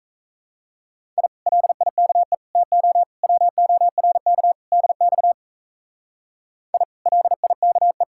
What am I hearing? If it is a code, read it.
Morse code, heard as a keyed tone, about 37 wpm, 700 Hz. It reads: I LIKE TO WORK DX I LIKE